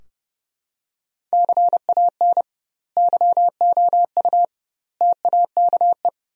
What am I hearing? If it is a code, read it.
Morse code, heard as a keyed tone, about 30 wpm, 700 Hz. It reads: CAN YOU TAKE
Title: 2x Word-Spacing - Sentences from Top 200 Words - 30wpm